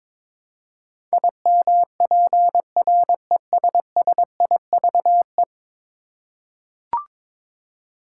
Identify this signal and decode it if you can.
Morse code, heard as a keyed tone, about 22 wpm, 700 Hz. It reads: IMPRESSIVE
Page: morse-code-ninja-practice-22wpm